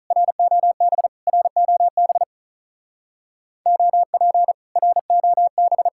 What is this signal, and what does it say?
Morse code, heard as a keyed tone, about 47 wpm, 700 Hz. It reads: ROB ROB OP ROB